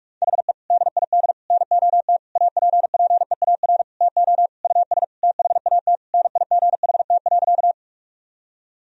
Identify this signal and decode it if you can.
Morse code, heard as a keyed tone, about 46 wpm, 700 Hz. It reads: HE DID NOT APPEAR TO US THAT NIGHT.
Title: Sentences from Top 1000 Words - Review Part 2 - 45wpm